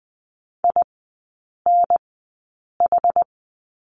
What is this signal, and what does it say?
Morse code, heard as a keyed tone, about 20 wpm, spaced as for 8 wpm, 700 Hz. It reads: INH
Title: 060 Lesson 11e - Letter h - Three Characters 20wpm